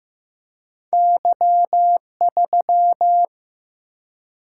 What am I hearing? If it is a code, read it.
Morse code, heard as a keyed tone, about 15 wpm, 700 Hz. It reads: Y3
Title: Single-Letter Single-Number 15wpm